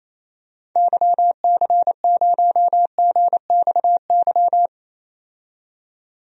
Morse code, heard as a keyed tone, about 28 wpm, 700 Hz. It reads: YC0GXY
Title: DXCC-100 Call Signs - 6 Characters - 28wpm